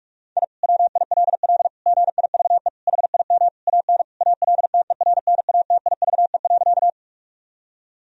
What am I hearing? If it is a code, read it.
Morse code, heard as a keyed tone, about 45 wpm, 700 Hz. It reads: I WILL GIVE HIM AN ALTERNATIVE.
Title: Sentences from Top 1000 Words - Review Part 4 - 45wpm